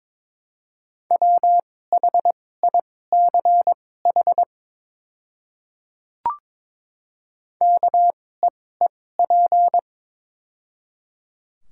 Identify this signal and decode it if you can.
Morse code, heard as a keyed tone, about 22 wpm, spaced as for 16 wpm, 700 Hz. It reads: WHICH KEEP
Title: 2x Character Spacing - Top 500 Words - 22wpm